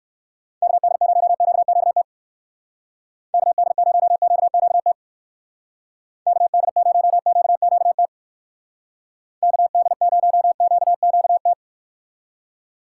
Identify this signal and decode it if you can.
Morse code, heard as a keyed tone, about 59 wpm, 700 Hz. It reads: KD0QQT KD0QQT KD0QQT KD0QQT